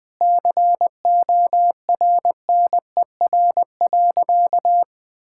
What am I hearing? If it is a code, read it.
Morse code, heard as a keyed tone, about 19 wpm, 700 Hz. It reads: CORNER.
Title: Sentences from Top 1600 Words 20wpm